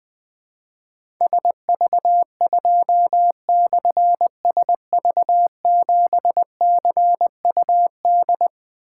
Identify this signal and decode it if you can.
Morse code, heard as a keyed tone, about 21 wpm, 700 Hz. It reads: SV2/SV7CUD
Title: Difficult Call Signs 20wpm